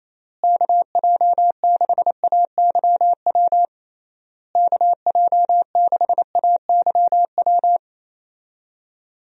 Morse code, heard as a keyed tone, about 28 wpm, 700 Hz. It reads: KJ6AYW KJ6AYW
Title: USA Call Signs - 6 Characters - Repeated 3x - 28wpm